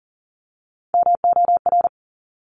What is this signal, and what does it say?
Morse code, heard as a keyed tone, about 38 wpm, 700 Hz. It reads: MOR
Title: Top Three Letter Combinations in Words 40wpm